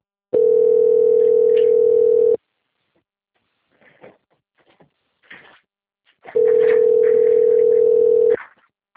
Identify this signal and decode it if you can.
Ringback tone (North America), 440 + 480 Hz.